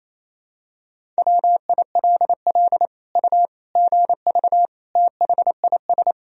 Morse code, heard as a keyed tone, about 27 wpm, 700 Hz. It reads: WILL U GV THIS
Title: Sentences from Top 200 Words and Abbreviations 28wpm